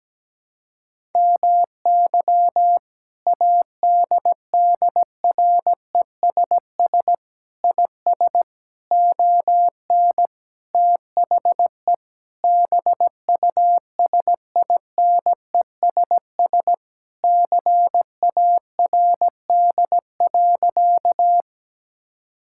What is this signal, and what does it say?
Morse code, heard as a keyed tone, about 17 wpm, 700 Hz. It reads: MY ADDRESS IS ON THE BUSINESS CARD.
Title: Sentences from Top 1000 Words - Review Part 6 - 17wpm